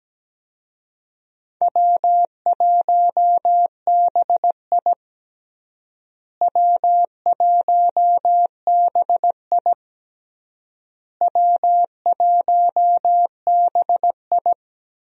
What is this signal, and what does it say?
Morse code, heard as a keyed tone, about 17 wpm, 700 Hz. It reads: W1BI W1BI W1BI
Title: USA Call Signs - 4 Characters - Repeated 3x - 17wpm